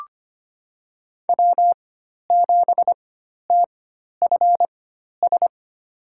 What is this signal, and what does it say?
Morse code, heard as a keyed tone, about 25 wpm, spaced as for 12 wpm, 700 Hz. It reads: W7TFS
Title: USA Call Signs - 5 Characters - 4x Character Spacing - 25wpm